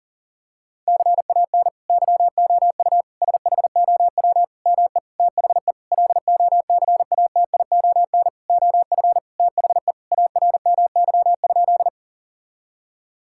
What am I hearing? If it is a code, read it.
Morse code, heard as a keyed tone, about 40 wpm, 700 Hz. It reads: CAN YOU SHOW ME THE LOCATION OF THE ARMY?
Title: Sentences from Top 1000 Words - Review Part 6 - 40wpm